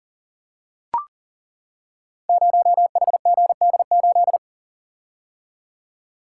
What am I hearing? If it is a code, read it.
Morse code, heard as a keyed tone, about 40 wpm, 700 Hz. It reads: 0HGD8
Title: Code Groups - 5 Letter:Numbers - 40wpm